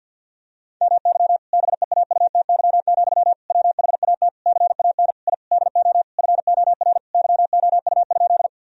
Morse code, heard as a keyed tone, about 50 wpm, 700 Hz. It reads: MY BEAUTY, WHAT CAN I DO FOR YOU?
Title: Sentences from Top 1000 Words - Review Part 2 - 50wpm